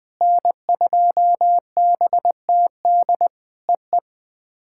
Morse code, heard as a keyed tone, about 20 wpm, 700 Hz. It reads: N2BTD EE
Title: POTA QSO Practice - 20wpm